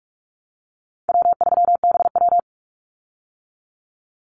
Morse code, heard as a keyed tone, about 44 wpm, 700 Hz. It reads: W3BW
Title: Easy Call Signs - 45wpm